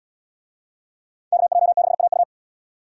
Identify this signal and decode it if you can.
Morse code, heard as a keyed tone, about 77 wpm, 700 Hz. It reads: KJ6AU